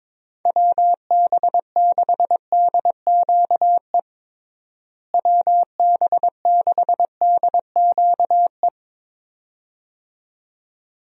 Morse code, heard as a keyed tone, about 22 wpm, 700 Hz. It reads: WB6DQE WB6DQE
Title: USA Call Signs - 6 Characters - Repeated 2x - 22wpm